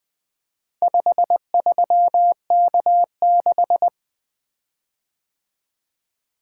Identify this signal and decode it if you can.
Morse code, heard as a keyed tone, about 20 wpm, 700 Hz. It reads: 53K6
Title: Four Letter-Number Combinations - 20wpm